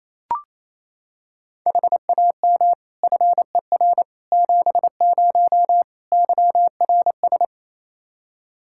Morse code, heard as a keyed tone, about 28 wpm, 700 Hz. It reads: HAM FER 70 YRS
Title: Practice QSO - Exchange 3 - More Info - 28wpm